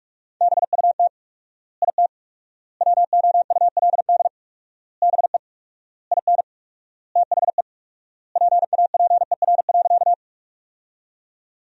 Morse code, heard as a keyed tone, about 46 wpm, 700 Hz. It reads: BUT IT WOULD BE IN THE PAPER.
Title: 4x Word-Spacing - Sentences from Top 300 Words - 45wpm